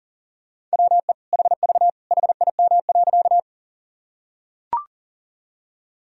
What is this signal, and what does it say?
Morse code, heard as a keyed tone, about 40 wpm, 700 Hz. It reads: WE HV HIM.
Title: Sentences from Top 100 Words and Abbreviations 40wpm